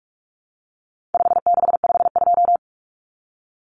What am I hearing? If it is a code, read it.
Morse code, heard as a keyed tone, about 44 wpm, 700 Hz. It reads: 5652
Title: Common Cut Numbers - 4-Digit 45wpm